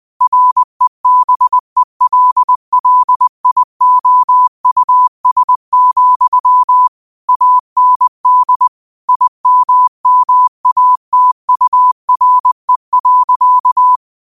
Morse code, heard as a keyed tone, about 20 wpm, 1000 Hz. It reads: REBELLIOUS, AND IMMATURE.
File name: qotdfiles / 20wpm